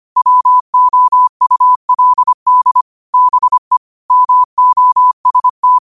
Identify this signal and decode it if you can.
Morse code, heard as a keyed tone, about 25 wpm, 1000 Hz. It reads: WOULD BE MOST
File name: 25wpm